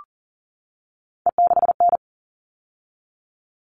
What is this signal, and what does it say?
Morse code, heard as a keyed tone, about 40 wpm, 700 Hz. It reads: E6N
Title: All Cut Numbers - 3-Digit 40wpm